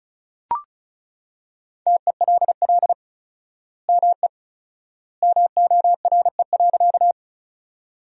Morse code, heard as a keyed tone, about 34 wpm, 700 Hz. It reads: TELL ME MORE.
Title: 4x Word-Spacing - Sentences from Top 300 Words - 35wpm